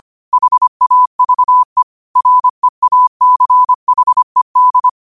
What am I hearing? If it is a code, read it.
Morse code, heard as a keyed tone, about 25 wpm, 1000 Hz. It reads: HAVE REACHED
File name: qotdfiles / 25wpm